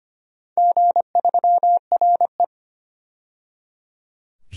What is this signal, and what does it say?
Morse code, heard as a keyed tone, about 25 wpm, 700 Hz. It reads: G3RE
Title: DXCC-100 Call Signs - 4 Characters - 25wpm